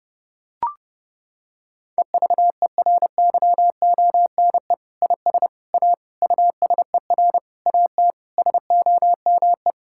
Morse code, heard as a keyed tone, about 30 wpm, 700 Hz. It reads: EVERYONE IS A USER AT SOME
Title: Sentences from Top 1000 Words - Review Part 5 - 30wpm